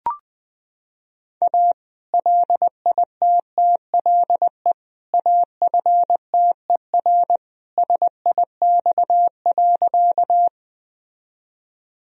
Morse code, heard as a keyed tone, about 20 wpm, 700 Hz. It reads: A LITTLE AFTER SIX.
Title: Sentences from Top 1000 Words - Review Part 2 - 20wpm